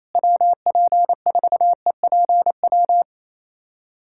Morse code, heard as a keyed tone, about 28 wpm, 700 Hz. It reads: WP4EPW
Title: USA Callsign DE Callsign - 6 Characters - 28wpm